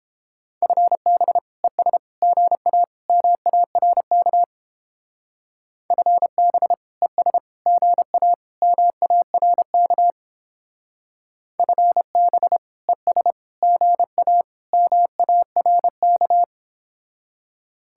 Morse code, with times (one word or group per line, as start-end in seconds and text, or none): 0.6-1.4 FB
1.6-2.0 ES
2.2-2.8 GA
3.1-4.4 MARK
5.9-6.7 FB
7.0-7.4 ES
7.7-8.3 GA
8.6-10.1 MARK
11.6-12.6 FB
12.9-13.3 ES
13.6-14.4 GA
14.7-16.4 MARK